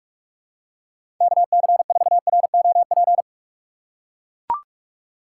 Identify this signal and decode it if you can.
Morse code, heard as a keyed tone, about 42 wpm, 700 Hz. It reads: KC4ROP
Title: USA Call Signs - 6 Characters - 4x Character Spacing - 45wpm